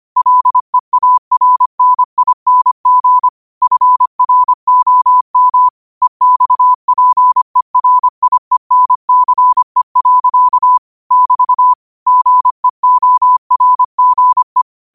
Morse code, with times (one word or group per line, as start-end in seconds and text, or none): 0.2-3.3 LEARNING
3.6-5.7 FROM
6.0-10.8 EXPERIENCE.
11.1-11.7 <BT>
12.1-14.6 GEORGE